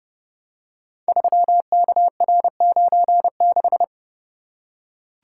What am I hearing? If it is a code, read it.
Morse code, heard as a keyed tone, about 30 wpm, 700 Hz. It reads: 3KR96
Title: Code Groups - 5 Letter:Numbers - 30wpm